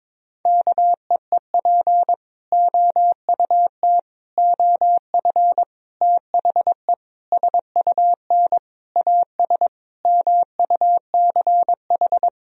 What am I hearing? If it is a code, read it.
Morse code, heard as a keyed tone, about 22 wpm, 700 Hz. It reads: KEEP OUT OF THE SUN AS MUCH
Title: Sentences from Top 1000 Words - Review Part 1 - 22wpm